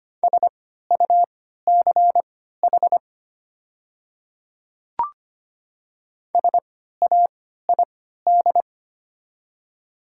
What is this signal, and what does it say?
Morse code, heard as a keyed tone, about 25 wpm, spaced as for 14 wpm, 700 Hz. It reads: SUCH SAID
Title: 3x Character Spacing - Top 300 Words - 25wpm